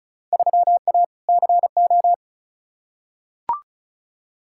Morse code, heard as a keyed tone, about 36 wpm, 690 Hz. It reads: 3A CO
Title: Field Day Exchanges Encore - 35wpm